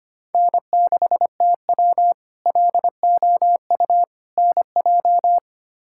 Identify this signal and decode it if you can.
Morse code, heard as a keyed tone, about 25 wpm, 700 Hz. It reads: N6TW LOU NJ
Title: K1USN SST QSO Practice - 25wpm